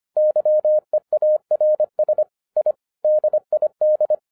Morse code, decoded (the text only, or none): YEARS I DID